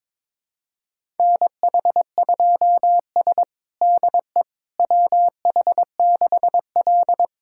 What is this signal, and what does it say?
Morse code, heard as a keyed tone, about 22 wpm, 700 Hz. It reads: NH2S DE WH6L
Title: USA Callsign DE Callsign - 4 Characters - 22wpm